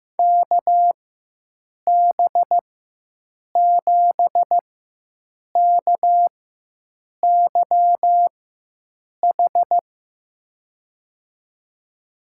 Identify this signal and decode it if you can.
Morse code, heard as a keyed tone, about 15 wpm, spaced as for 7 wpm, 700 Hz. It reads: KB7KYH
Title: USA Call Signs - 6 Characters - 4x Character Spacing - 15wpm